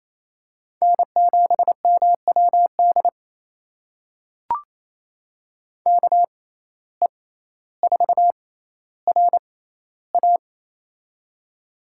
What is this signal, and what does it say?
Morse code, heard as a keyed tone, about 27 wpm, spaced as for 14 wpm, 700 Hz. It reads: N7MWD KE4RA
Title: USA Call Signs - 5 Characters - 6x Character Spacing - 28wpm